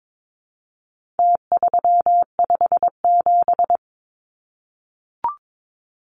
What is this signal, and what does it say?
Morse code, heard as a keyed tone, about 22 wpm, 700 Hz. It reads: T357
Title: Numbers - Contest Style - 4-Digit 22wpm